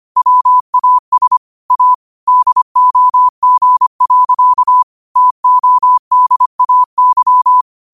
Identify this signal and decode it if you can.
Morse code, heard as a keyed tone, about 25 wpm, 1000 Hz. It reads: WAS A DOG. TODAY
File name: qotdfiles / 25wpm